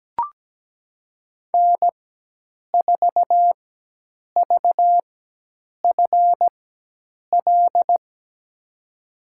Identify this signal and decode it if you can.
Morse code, heard as a keyed tone, about 17 wpm, spaced as for 8 wpm, 700 Hz. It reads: N4VFL
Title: USA Call Signs - 5 Characters - 4x Character Spacing - 17wpm